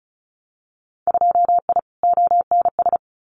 Morse code, heard as a keyed tone, about 34 wpm, 700 Hz. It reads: 2I ONS
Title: Winter Field Day Exchanges Encore 35wpm